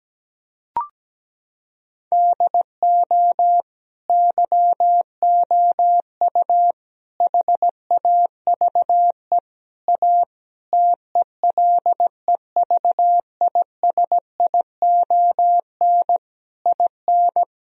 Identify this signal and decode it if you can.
Morse code, heard as a keyed tone, about 17 wpm, 700 Hz. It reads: DO YOU HAVE A TELEVISION IN
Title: Sentences from Top 1000 Words - Review Part 4 - 17wpm